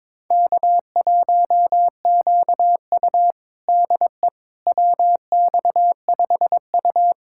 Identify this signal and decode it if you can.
Morse code, heard as a keyed tone, about 22 wpm, 700 Hz. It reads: K1QU DE WX5U
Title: USA Callsign DE Callsign - 4 Characters - 22wpm